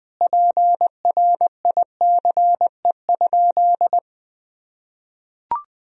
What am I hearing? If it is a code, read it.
Morse code, heard as a keyed tone, about 20 wpm, 700 Hz. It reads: PRICE?
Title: Sentences from Top 1000 Words - Review Part 3 - 20wpm